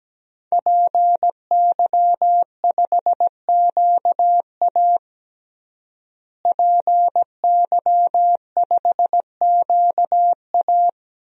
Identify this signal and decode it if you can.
Morse code, heard as a keyed tone, about 17 wpm, 700 Hz. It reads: PY5QA PY5QA
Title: DXCC-100 Call Signs - 5 Characters - Repeated 3x - 17wpm